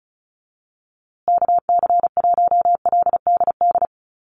Morse code, heard as a keyed tone, about 34 wpm, 700 Hz. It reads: KC1LDD
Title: USA Call Signs Repeated 2x 35wpm